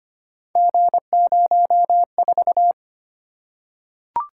Morse code, heard as a keyed tone, about 25 wpm, 700 Hz. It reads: G04
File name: Three Letter-Number Combinations - 25wpm